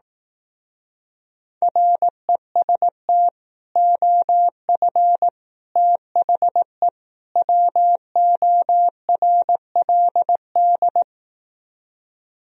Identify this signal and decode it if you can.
Morse code, heard as a keyed tone, about 18 wpm, 700 Hz. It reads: REST OF THE WORLD
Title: Speed-Racing - Sets of 4 Words 15wpm